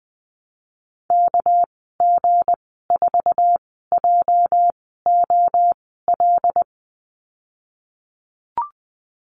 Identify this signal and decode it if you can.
Morse code, heard as a keyed tone, about 20 wpm, spaced as for 15 wpm, 700 Hz. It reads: KG4JOL
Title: 207 Lesson 33g - Letter j - Call Signs 20wpm